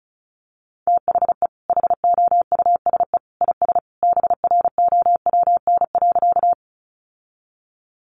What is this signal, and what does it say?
Morse code, heard as a keyed tone, about 34 wpm, 700 Hz. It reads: THE HOUSE IS BROWN.
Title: Sentences from Top 1200 Words 35wpm